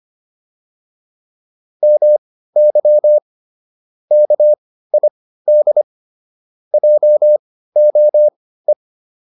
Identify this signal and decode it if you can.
Morse code, heard as a keyed tone, about 25 wpm, spaced as for 15 wpm, 600 Hz. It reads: MY KID JOE